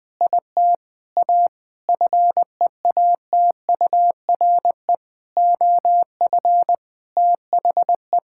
Morse code, heard as a keyed tone, about 20 wpm, 700 Hz. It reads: IT A FEATURE OF THE
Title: Sentences from Top 1000 Words - Review Part 5 - 20wpm